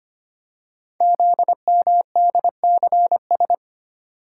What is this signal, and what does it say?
Morse code, heard as a keyed tone, about 25 wpm, 700 Hz. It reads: ZMDCS
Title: Code Groups - 5 Letters - 25wpm